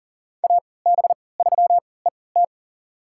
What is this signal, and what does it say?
Morse code, heard as a keyed tone, about 40 wpm, spaced as for 23 wpm, 700 Hz. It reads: AB3ET
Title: USA Call Signs - 5 Characters - 3x Character Spacing - 40wpm